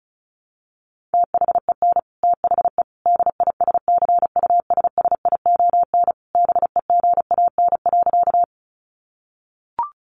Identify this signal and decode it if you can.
Morse code, heard as a keyed tone, about 34 wpm, 700 Hz. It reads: THEN THE DISCUSSION BEGAN.
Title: Sentences from Top 1100 Words 35wpm